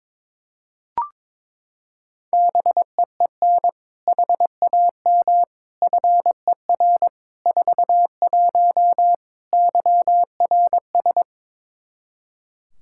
Morse code, heard as a keyed tone, about 23 wpm, 700 Hz. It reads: BEEN HAM FER 41 YRS
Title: Practice QSO - Exchange 3 - More Info - 22wpm